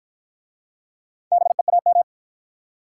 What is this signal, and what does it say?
Morse code, heard as a keyed tone, about 53 wpm, 700 Hz. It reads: BEAM